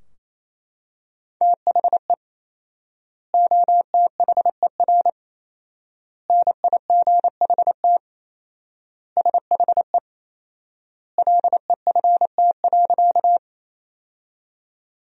Morse code, heard as a keyed tone, about 27 wpm, 700 Hz. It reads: THE OTHER NIGHT SHE LEFT.
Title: 4x Word-Spacing - Sentences from Top 300 Words - 28wpm